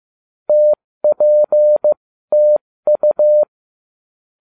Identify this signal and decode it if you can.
Morse code, heard as a keyed tone, about 15 wpm, spaced as for 13 wpm, 600 Hz. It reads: TPTU